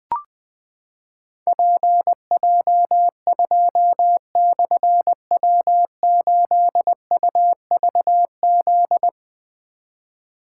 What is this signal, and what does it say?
Morse code, heard as a keyed tone, about 20 wpm, 700 Hz. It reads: PJ2/W8UVZ
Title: Difficult Call Signs 20wpm